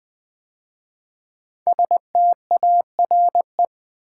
Morse code, heard as a keyed tone, about 20 wpm, 700 Hz. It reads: STARE
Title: 5 Letter Words - First 3 Letters Given - 20wpm